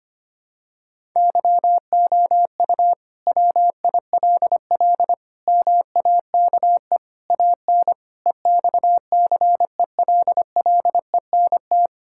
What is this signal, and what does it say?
Morse code, heard as a keyed tone, about 25 wpm, 700 Hz. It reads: YOU WILL MAKE AN EXCELLENT
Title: Sentences from Top 1000 Words - Review Part 5 - 25wpm